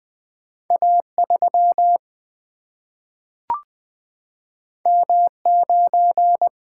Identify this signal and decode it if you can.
Morse code, heard as a keyed tone, about 20 wpm, 700 Hz. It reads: A 3 M 9
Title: Single-Letter Single-Number 20wpm